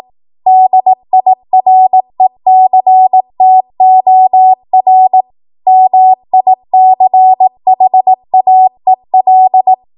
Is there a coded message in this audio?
Yes: Morse code from 0.5 to 9.7 s, heard as a keyed tone, about 18 wpm, 750 Hz. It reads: DIRECTOR MICHAEL